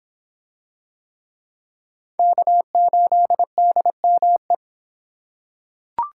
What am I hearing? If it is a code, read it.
Morse code, heard as a keyed tone, about 26 wpm, 700 Hz. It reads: K8DME